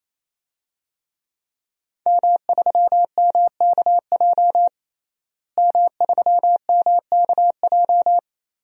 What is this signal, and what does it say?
Morse code, heard as a keyed tone, about 28 wpm, 700 Hz. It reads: M3MKJ M3MKJ
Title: DXCC-100 Call Signs - 5 Characters - Repeated 3x - 28wpm